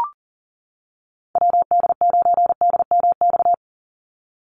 Morse code, heard as a keyed tone, about 40 wpm, 700 Hz. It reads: WD9DMX
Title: USA Call Signs 40wpm